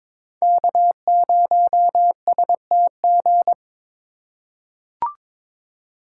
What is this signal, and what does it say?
Morse code, heard as a keyed tone, about 22 wpm, 700 Hz. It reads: K0STG
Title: USA Call Signs - 5 Characters - 3x Character Spacing - 22wpm